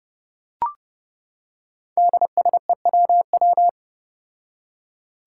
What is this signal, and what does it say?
Morse code, heard as a keyed tone, about 30 wpm, 700 Hz. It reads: DSEWW
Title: Five Challenging Letter-Number Combinations 30wpm